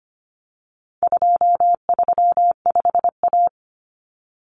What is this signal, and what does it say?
Morse code, heard as a keyed tone, about 25 wpm, 700 Hz. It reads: 235A
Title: Common Cut Numbers - 4-Digit 25wpm